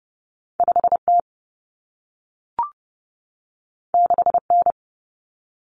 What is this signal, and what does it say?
Morse code, heard as a keyed tone, about 30 wpm, 700 Hz. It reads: 5T 6N
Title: Common Cut Numbers - 2-Digit 30wpm